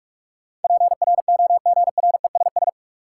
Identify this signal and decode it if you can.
Morse code, heard as a keyed tone, about 43 wpm, 700 Hz. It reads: PROGRESS
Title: Sentences from Top 1400 Words - New Word Drill Alt - 45wpm